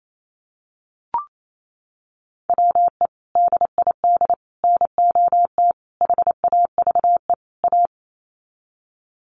Morse code, heard as a keyed tone, about 28 wpm, 700 Hz. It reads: WE DID NOT HAVE A
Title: Sets of 5 Words Encore2 28wpm